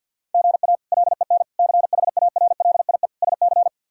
Morse code, heard as a keyed tone, about 50 wpm, 700 Hz. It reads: GA LEN CHARLIE SC
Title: K1USN SST QSO Practice - 50wpm